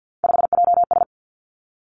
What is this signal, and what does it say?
Morse code, heard as a keyed tone, about 50 wpm, 700 Hz. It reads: 5JS
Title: Three Challenging Letter-Number Combinations 50wpm